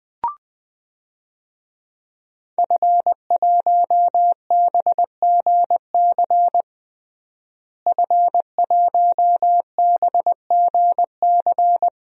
Morse code, heard as a keyed tone, about 21 wpm, 700 Hz. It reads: F1BGC F1BGC
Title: DXCC-100 Call Signs - 5 Characters - Repeated 3x - 20wpm